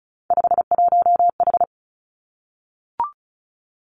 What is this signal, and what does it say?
Morse code, heard as a keyed tone, about 36 wpm, 700 Hz. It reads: 51H
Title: Three Challenging Letter-Number Combinations 35wpm